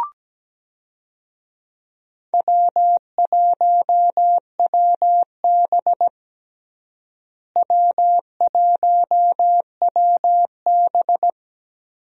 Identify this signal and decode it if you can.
Morse code, heard as a keyed tone, about 17 wpm, 700 Hz. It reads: W1WB W1WB
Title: USA Call Signs - 4 Characters - Repeated 3x - 17wpm